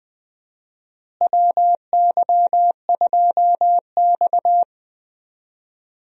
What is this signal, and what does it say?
Morse code, heard as a keyed tone, about 20 wpm, 700 Hz. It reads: WY2X
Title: USA Call Signs - 4 Characters - Repeated 2x - 20wpm